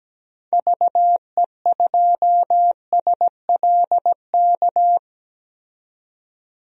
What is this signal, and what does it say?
Morse code, heard as a keyed tone, about 17 wpm, 700 Hz. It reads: VE2SLK
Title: DXCC-100 Call Signs - 6 Characters - 17wpm